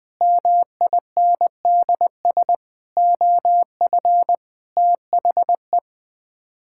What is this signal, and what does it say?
Morse code, heard as a keyed tone, about 21 wpm, 700 Hz. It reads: MINDS OF THE
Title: Sets of 8 Words - 20wpm